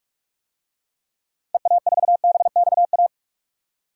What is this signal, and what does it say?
Morse code, heard as a keyed tone, about 42 wpm, 700 Hz. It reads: EA4BXA
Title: DXCC-100 Call Signs - 6 Characters - 45wpm